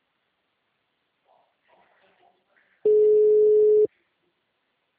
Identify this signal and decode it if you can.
Ringback tone (Europe, 425 Hz).
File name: out- -2021- - - .40519